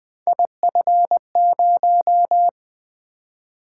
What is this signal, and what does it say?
Morse code, heard as a keyed tone, about 20 wpm, 700 Hz. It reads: IF0
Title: Three Letter-Number Combinations - 20wpm